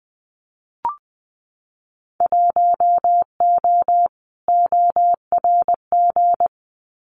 Morse code, heard as a keyed tone, about 20 wpm, 700 Hz. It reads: 1O ORG
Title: Winter Field Day Exchanges 20wpm